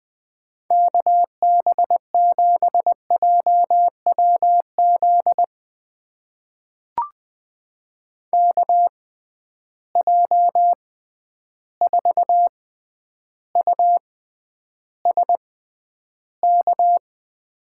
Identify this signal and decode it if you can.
Morse code, heard as a keyed tone, about 20 wpm, spaced as for 10 wpm, 700 Hz. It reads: KB7JWZ KJ4USK
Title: USA Call Signs - 6 Characters - 6x Character Spacing - 20wpm